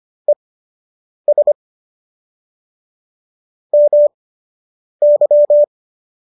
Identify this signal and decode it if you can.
Morse code, heard as a keyed tone, about 25 wpm, spaced as for 8 wpm, 590 Hz. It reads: ES MY